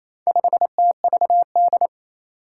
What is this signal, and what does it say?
Morse code, heard as a keyed tone, about 28 wpm, 700 Hz. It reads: 5TVD
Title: Four Challenging Letter-Number Combinations 28wpm